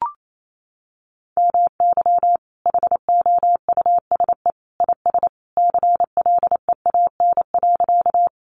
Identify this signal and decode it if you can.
Morse code, heard as a keyed tone, about 28 wpm, 700 Hz. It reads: MY HOUSE IS CLEAN.
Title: Sentences from Top 1100 Words 28wpm